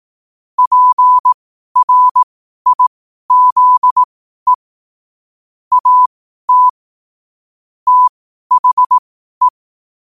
Morse code, heard as a keyed tone, about 18 wpm, spaced as for 12 wpm, 1000 Hz. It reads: PRIZE AT THE
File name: qotdfiles / 12wpm